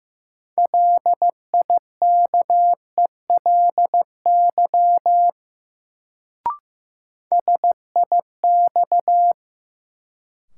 Morse code, heard as a keyed tone, about 15 wpm, 700 Hz. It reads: LIKELY SIX
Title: Sentences from Top 600 Words - New Word Drill - 15wpm